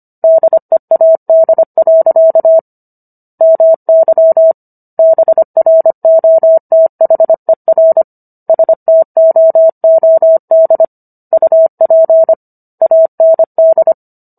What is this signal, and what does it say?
Morse code, heard as a keyed tone, about 25 wpm, 650 Hz. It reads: DEAD. MY BROTHER STOOD UP AND